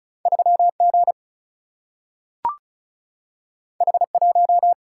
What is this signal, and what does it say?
Morse code, heard as a keyed tone, about 36 wpm, 700 Hz. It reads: 3 G H 1
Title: Single-Letter Single-Number 35wpm